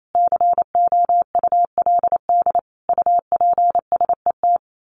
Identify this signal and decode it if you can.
Morse code, heard as a keyed tone, about 28 wpm, 700 Hz. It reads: COULD UPSET
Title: A Sound Of Thunder 28wpm